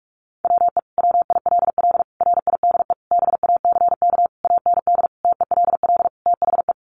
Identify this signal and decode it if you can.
Morse code, heard as a keyed tone, about 44 wpm, 700 Hz. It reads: WE WILL RIDE BACK AND TELL THE
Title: Sentences from Top 1200 Words 45wpm